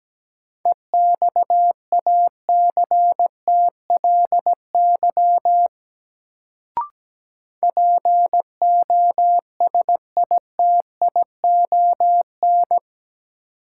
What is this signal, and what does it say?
Morse code, heard as a keyed tone, about 17 wpm, 700 Hz. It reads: EXACTLY POSITION
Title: Sentences from Top 900 Words - New Word Drill - 17wpm